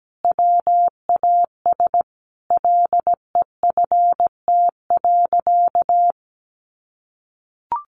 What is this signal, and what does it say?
Morse code, heard as a keyed tone, about 17 wpm, 700 Hz. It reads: WAS LEFT.
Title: Sentences from Top 1200 Words 17wpm